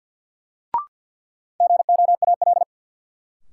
Morse code, heard as a keyed tone, about 53 wpm, 700 Hz. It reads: GOAL
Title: Sentences from Top 1400 Words - New Word Drill Alt - 50wpm